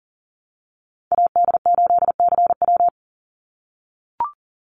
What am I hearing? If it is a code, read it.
Morse code, heard as a keyed tone, about 40 wpm, 700 Hz. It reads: AD8CW
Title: Call Signs - 40wpm